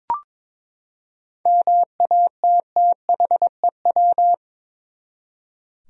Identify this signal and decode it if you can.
Morse code, heard as a keyed tone, about 21 wpm, 700 Hz. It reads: MATTHEW
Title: USA Common Names - 22wpm